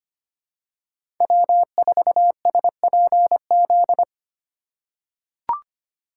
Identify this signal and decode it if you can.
Morse code, heard as a keyed tone, about 25 wpm, 700 Hz. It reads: W4SPZ
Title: USA Call Signs - 5 Characters - 2x Character Spacing - 25wpm